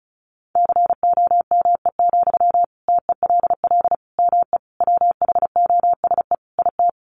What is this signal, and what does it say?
Morse code, heard as a keyed tone, about 34 wpm, 700 Hz. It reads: COME, TELL ME WHOSE IT
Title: Sentences from Top 1200 Words 35wpm